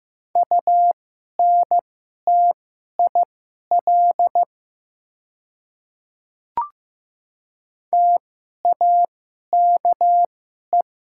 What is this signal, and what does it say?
Morse code, heard as a keyed tone, about 15 wpm, spaced as for 11 wpm, 700 Hz. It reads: UNTIL TAKE
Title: 2x Character Spacing - Top 300 Words - 15wpm